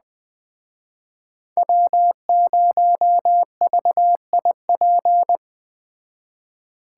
Morse code, heard as a keyed tone, about 19 wpm, 700 Hz. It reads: W0VIP
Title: Speed Racing - USA 1x3 Call Signs 17wpm